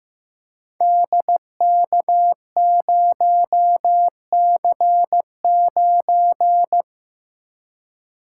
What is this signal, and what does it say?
Morse code, heard as a keyed tone, about 15 wpm, 700 Hz. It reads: DK0C9
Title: Code Groups - 5 Letter:Numbers - 15wpm